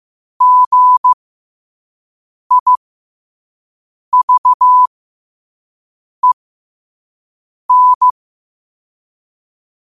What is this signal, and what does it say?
Morse code, heard as a keyed tone, about 15 wpm, spaced as for 5 wpm, 1000 Hz. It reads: GIVEN
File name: qotdfiles / 5wpm